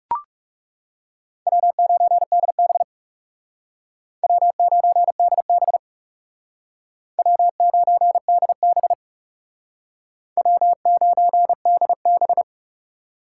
Morse code, heard as a keyed tone, about 38 wpm, 700 Hz. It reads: W9DB W9DB W9DB W9DB